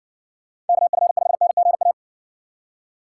Morse code, heard as a keyed tone, about 60 wpm, 700 Hz. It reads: KF5NGA